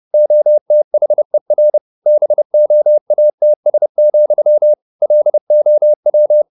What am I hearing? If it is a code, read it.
Morse code, heard as a keyed tone, about 30 wpm, 600 Hz. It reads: OTHER BOATS, LOW
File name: the old man and the sea / the old man and the sea 1 30wpm